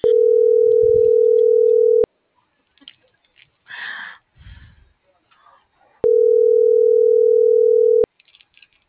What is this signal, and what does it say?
Ringback tone (North America), 440 + 480 Hz.
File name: CENTRAL GENERAL / out rubyalejandra